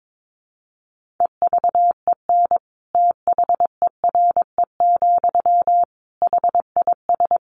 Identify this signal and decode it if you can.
Morse code, heard as a keyed tone, about 22 wpm, 700 Hz. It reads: EVEN THERE, HIS
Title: Sentences from Top 1500 Words 22wpm